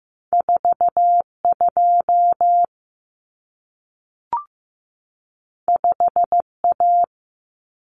Morse code, heard as a keyed tone, about 15 wpm, 700 Hz. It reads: 42 5A
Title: Common Cut Numbers - 2-Digit 15wpm